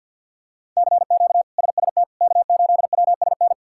Morse code, heard as a keyed tone, about 50 wpm, 700 Hz. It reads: CQ SST K8WSN
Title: K1USN SST QSO Practice - 50wpm